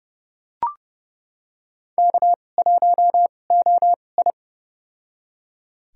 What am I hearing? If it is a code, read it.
Morse code, heard as a keyed tone, about 31 wpm, spaced as for 22 wpm, 700 Hz. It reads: K1OI
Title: USA Call Signs - 2x Character Spacing - 30wpm